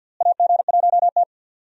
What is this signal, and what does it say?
Morse code, heard as a keyed tone, about 47 wpm, 700 Hz. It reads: AG1T
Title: USA Call Signs - 4 Characters - Repeated 3x - 50wpm